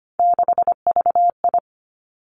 Morse code, heard as a keyed tone, about 25 wpm, 700 Hz. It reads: 6VI
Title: Three Challenging Letter-Number Combinations 25wpm